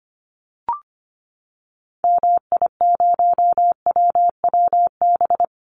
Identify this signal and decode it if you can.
Morse code, heard as a keyed tone, about 25 wpm, 700 Hz. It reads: MI0WWB
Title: Call Signs Repeated 2x - 25wpm